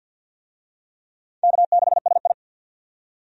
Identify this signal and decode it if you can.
Morse code, heard as a keyed tone, about 48 wpm, 700 Hz. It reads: K6SI
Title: USA Call Signs - 4 Characters - 50wpm